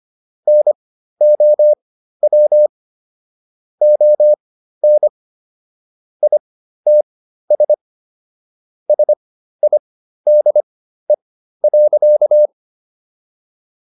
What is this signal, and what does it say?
Morse code, heard as a keyed tone, about 25 wpm, spaced as for 13 wpm, 600 Hz. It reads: NOW ON ITS SIDE.